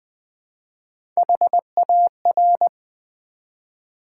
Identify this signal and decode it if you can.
Morse code, heard as a keyed tone, about 20 wpm, 700 Hz. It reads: HAR
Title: Build A Word - 1 Letter At A Time - 4-Letter Words - 20wpm